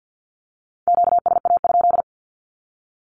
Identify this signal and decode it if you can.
Morse code, heard as a keyed tone, about 50 wpm, 700 Hz. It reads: QSA?
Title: Q-Codes with Spoken Meaning - 50wpm